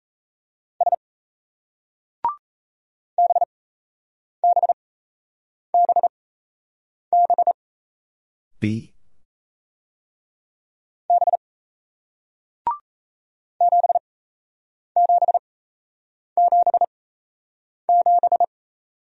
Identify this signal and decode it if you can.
Morse code, heard as a keyed tone, about 37 wpm, spaced as for 8 wpm, 700 Hz. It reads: S BBBB B 7777